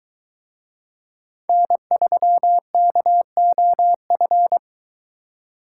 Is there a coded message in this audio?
Yes: Morse code from 1.5 to 4.6 s, heard as a keyed tone, about 23 wpm, 700 Hz. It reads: N3KOF